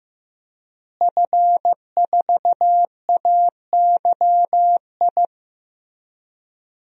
Morse code, heard as a keyed tone, about 15 wpm, 700 Hz. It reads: F4AYI
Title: DXCC-100 Call Signs - 5 Characters - Repeated 2x - 15wpm